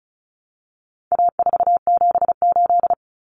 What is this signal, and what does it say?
Morse code, heard as a keyed tone, about 34 wpm, 700 Hz. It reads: A478
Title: Common Cut Numbers - 4-Digit 35wpm